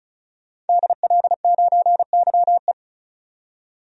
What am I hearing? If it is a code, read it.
Morse code, heard as a keyed tone, about 36 wpm, 700 Hz. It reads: DL9YE
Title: DXCC-100 Call Signs - 5 Characters - 35wpm